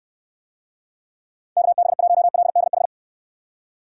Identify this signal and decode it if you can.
Morse code, heard as a keyed tone, about 69 wpm, 700 Hz. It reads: KB1LDH